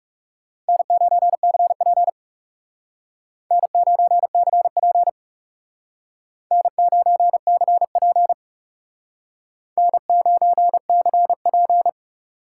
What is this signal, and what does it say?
Morse code, heard as a keyed tone, about 36 wpm, 700 Hz. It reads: N9CP N9CP N9CP N9CP
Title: Speed Racing - USA 1x2 Call Signs 30wpm